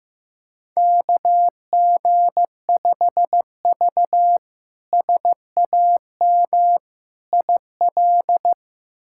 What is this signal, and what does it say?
Morse code, heard as a keyed tone, about 15 wpm, 700 Hz. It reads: KG5V SAM IL
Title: K1USN SST QSO Practice - 15wpm